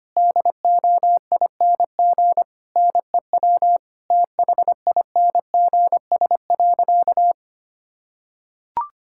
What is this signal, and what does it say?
Morse code, heard as a keyed tone, about 25 wpm, 700 Hz. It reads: DOING NEW THINGS.
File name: practice / 25wpm / Difficult Sentences with Hints - 25wpm